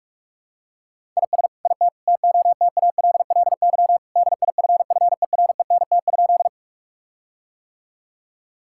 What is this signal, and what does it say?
Morse code, heard as a keyed tone, about 41 wpm, 700 Hz. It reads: IS IT TOTALLY DIFFERENT?
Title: Sentences from Top 1600 Words 45wpm